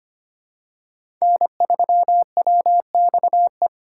Morse code, heard as a keyed tone, about 25 wpm, 700 Hz. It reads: N3WXE
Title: USA Call Signs - 2x Character Spacing - 25wpm